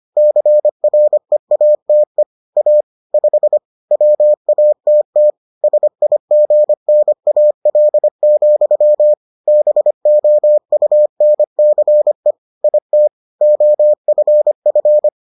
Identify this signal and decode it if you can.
Morse code, heard as a keyed tone, about 25 wpm, 600 Hz. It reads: CREATE A 5 WATT SIGNAL, BOUNCE IT OFF